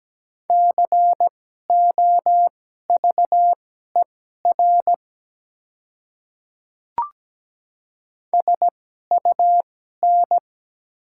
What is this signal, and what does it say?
Morse code, heard as a keyed tone, about 17 wpm, spaced as for 12 wpm, 700 Hz. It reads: COVER SUN
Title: 2x Character Spacing - Top 300 Words - 17wpm